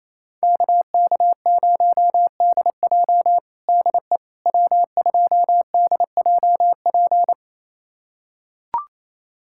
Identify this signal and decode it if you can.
Morse code, heard as a keyed tone, about 28 wpm, 700 Hz. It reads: KK0DJ DE W2DJP
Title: USA Callsign DE Callsign - 5 Characters - 28wpm